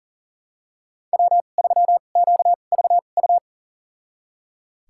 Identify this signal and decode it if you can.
Morse code, heard as a keyed tone, about 40 wpm, spaced as for 29 wpm, 700 Hz. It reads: W3QVU
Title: USA Call Signs - 5 Characters - 2x Character Spacing - 40wpm